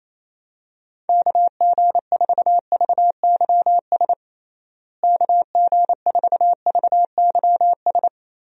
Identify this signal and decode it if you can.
Morse code, heard as a keyed tone, about 28 wpm, 700 Hz. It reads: KG4VYS KG4VYS
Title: USA Call Signs - 6 Characters - Repeated 2x - 28wpm